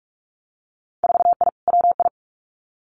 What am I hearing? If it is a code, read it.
Morse code, heard as a keyed tone, about 46 wpm, 700 Hz. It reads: 4I WI
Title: Winter Field Day Exchanges 45wpm